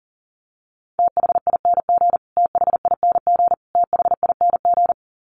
Morse code, heard as a keyed tone, about 40 wpm, 700 Hz. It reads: THING THING THING
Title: Top 500 Words Repeated 40wpm